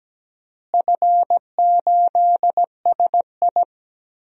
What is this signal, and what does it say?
Morse code, heard as a keyed tone, about 17 wpm, 700 Hz. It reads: F8SI
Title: DXCC-100 Call Signs - 4 Characters - Repeated 3x - 17wpm